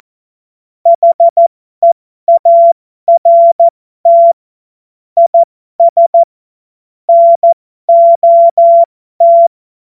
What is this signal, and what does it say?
Morse code, heard as a keyed tone, about 14 wpm, spaced as for 12 wpm, 680 Hz. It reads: HEART IS NOT
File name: Morsekurs / Witze